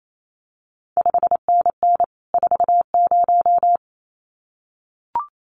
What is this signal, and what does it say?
Morse code, heard as a keyed tone, about 28 wpm, 700 Hz. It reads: 5NN 40
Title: CQ WW DX Contest Exchange Encore - 28wpm